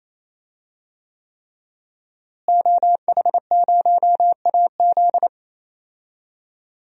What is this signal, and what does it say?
Morse code, heard as a keyed tone, about 28 wpm, 700 Hz. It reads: OH0AZ
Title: DXCC-100 Call Signs - 5 Characters - Repeated 3x - 28wpm